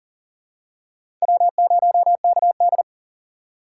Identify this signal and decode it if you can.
Morse code, heard as a keyed tone, about 40 wpm, 700 Hz. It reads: W0KD
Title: USA Call Signs - 4 Characters - 4x Character Spacing - 40wpm